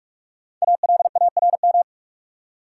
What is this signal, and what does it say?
Morse code, heard as a keyed tone, about 46 wpm, 700 Hz. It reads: ALARM
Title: Five Letter Words - 45wpm